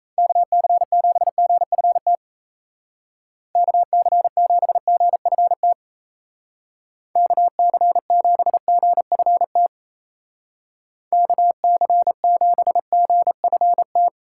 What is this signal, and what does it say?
Morse code, heard as a keyed tone, about 34 wpm, 700 Hz. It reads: KC7GFT KC7GFT KC7GFT KC7GFT
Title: Speed Racing - USA 2x3 Call Signs 28wpm